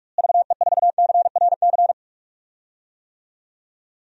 Morse code, heard as a keyed tone, about 42 wpm, 700 Hz. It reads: VE4CRC
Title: DXCC-100 Call Signs - 6 Characters - Repeated 2x - 45wpm